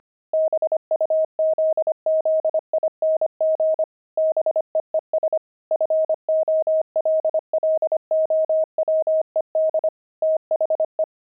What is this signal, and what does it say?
Morse code, heard as a keyed tone, about 25 wpm, 630 Hz. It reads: BUZZING BEES FOLLOWED THE
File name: pangram-fs-25-cs-25-beehive